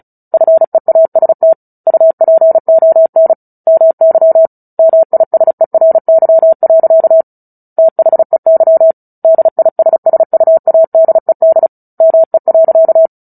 Morse code, heard as a keyed tone, about 34 wpm, 650 Hz. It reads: FEAST UPON MY MISERY. THEY DISSUADED ME.
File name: Book Two / 09-Chapter 9 - 35 WPM